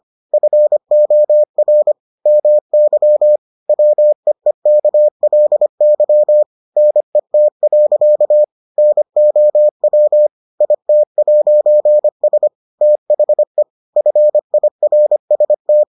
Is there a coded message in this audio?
Yes: Morse code from 0.3 to 15.8 s, heard as a keyed tone, about 25 wpm, 600 Hz. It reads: FOR MY WEEKLY NET. NOW IT'S THE FIRST